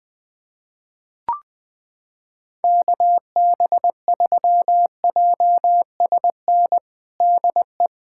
Morse code, heard as a keyed tone, about 20 wpm, 700 Hz. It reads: KB3JSN DE
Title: USA Callsign DE Callsign - 6 Characters - 20wpm